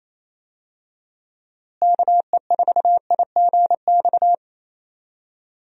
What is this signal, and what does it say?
Morse code, heard as a keyed tone, about 28 wpm, 700 Hz. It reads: KE4IGX
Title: USA Call Signs - 6 Characters - Repeated 3x - 28wpm